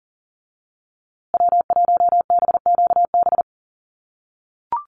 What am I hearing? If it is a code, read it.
Morse code, heard as a keyed tone, about 40 wpm, 700 Hz. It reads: W1BQB
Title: USA Call Signs Repeated 2x 40wpm